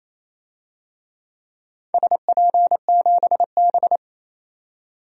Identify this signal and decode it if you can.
Morse code, heard as a keyed tone, about 28 wpm, 700 Hz. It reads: SP7B
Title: DXCC-100 Call Signs - 4 Characters - 28wpm